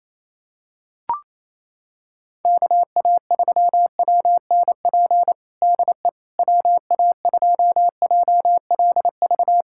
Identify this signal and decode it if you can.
Morse code, heard as a keyed tone, about 28 wpm, 700 Hz. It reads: KA3WNP DE WA2JLV
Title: USA Callsign DE Callsign - 6 Characters - 28wpm